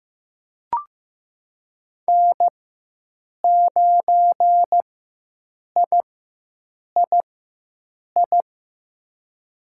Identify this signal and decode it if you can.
Morse code, heard as a keyed tone, about 15 wpm, spaced as for 7 wpm, 700 Hz. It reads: N9III
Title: USA Call Signs - 5 Characters - 4x Character Spacing - 15wpm